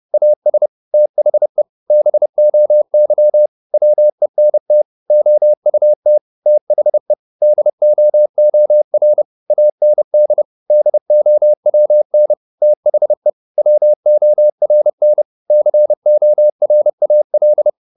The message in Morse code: AS THE BOY WENT OUT THE DOOR AND DOWN THE WORN CORAL

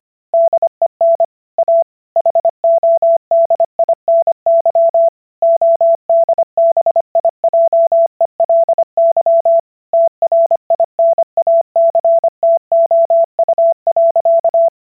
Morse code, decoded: DEN A HODINY ODBIJELY TRINACTOU.

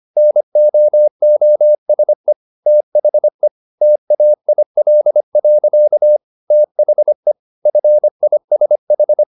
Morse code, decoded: NOOSE THE TAIL. THE FISH

